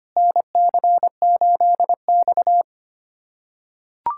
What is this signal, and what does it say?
Morse code, heard as a keyed tone, about 25 wpm, 700 Hz. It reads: NC8X
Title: USA Callsign DE Callsign - 4 Characters - 25wpm